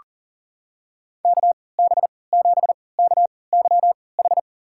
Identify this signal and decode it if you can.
Morse code, heard as a keyed tone, about 40 wpm, spaced as for 23 wpm, 700 Hz. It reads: KB7KYH
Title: USA Call Signs - 3x Character Spacing - 40wpm